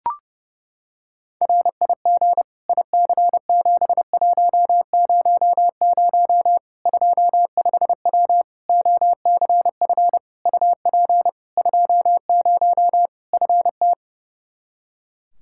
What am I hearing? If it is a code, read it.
Morse code, heard as a keyed tone, about 30 wpm, 700 Hz. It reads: RIG IC7100 25W OCF UP 20 FT
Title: K1USN SST QSO Practice 30wpm